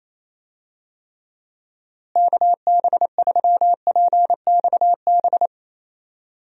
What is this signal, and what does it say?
Morse code, heard as a keyed tone, about 28 wpm, 700 Hz. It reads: KB3PXB
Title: USA Call Signs - 6 Characters - Repeated 2x - 28wpm